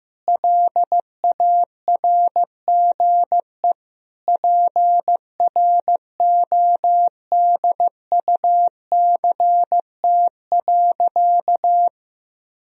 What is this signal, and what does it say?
Morse code, heard as a keyed tone, about 15 wpm, 700 Hz. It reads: LARGE PRODUCT.
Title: Sentences from Top 1000 Words - Review Part 2 - 15wpm